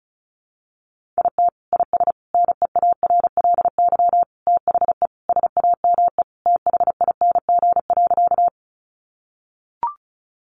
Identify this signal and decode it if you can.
Morse code, heard as a keyed tone, about 34 wpm, 700 Hz. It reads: IT IS NEARLY THE SAME THING.
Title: Sentences from Top 1300 Words 35wpm